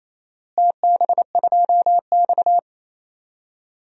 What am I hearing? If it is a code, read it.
Morse code, heard as a keyed tone, about 28 wpm, 700 Hz. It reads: TB2X
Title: DXCC-100 Call Signs - 4 Characters - Repeated 2x - 28wpm